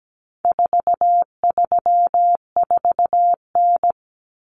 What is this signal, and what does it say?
Morse code, heard as a keyed tone, about 17 wpm, 700 Hz. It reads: 434N
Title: Common Cut Numbers - 4-Digit 17wpm